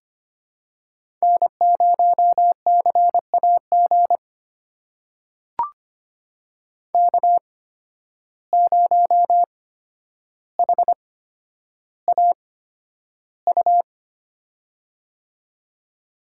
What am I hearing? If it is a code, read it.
Morse code, heard as a keyed tone, about 25 wpm, spaced as for 11 wpm, 700 Hz. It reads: N0CAG K0HAU
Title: USA Call Signs - 5 Characters - 8x Character Spacing - 25wpm